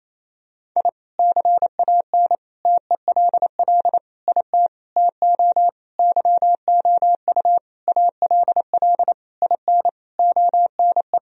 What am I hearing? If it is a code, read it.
Morse code, heard as a keyed tone, about 28 wpm, 700 Hz. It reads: I CAN TELL IT TO YOU ALL IN ONE
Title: Sentences from Top 1000 Words - Review Part 1 - 28wpm